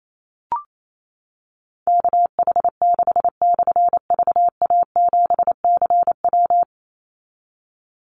Morse code, heard as a keyed tone, about 28 wpm, 700 Hz. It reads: KH6/VA7CW
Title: Call Signs - 28wpm